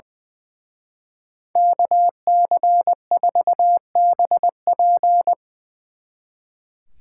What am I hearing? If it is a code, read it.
Morse code, heard as a keyed tone, about 23 wpm, 700 Hz. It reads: KC4BP
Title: Speed Racing - USA 2x2 Call Signs 20wpm